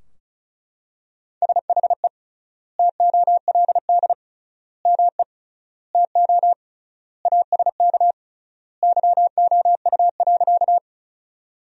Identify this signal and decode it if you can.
Morse code, heard as a keyed tone, about 36 wpm, 700 Hz. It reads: SHE TOLD ME TO ASK YOU.
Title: 3x Word-Spacing - Sentences from Top 400 Words - 35wpm